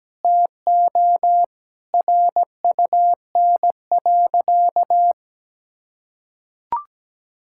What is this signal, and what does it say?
Morse code, heard as a keyed tone, about 17 wpm, 700 Hz. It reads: TO RUN.
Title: Sentences from Top 1000 Words - Review Part 5 - 17wpm